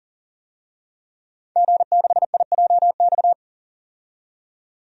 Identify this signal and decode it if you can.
Morse code, heard as a keyed tone, about 37 wpm, 700 Hz. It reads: G6IJX
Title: DXCC-100 Call Signs - 5 Characters - 40wpm